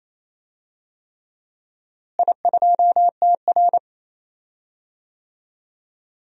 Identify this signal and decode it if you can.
Morse code, heard as a keyed tone, about 27 wpm, 700 Hz. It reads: I2TR
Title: DXCC-100 Call Signs - 4 Characters - Repeated 3x - 28wpm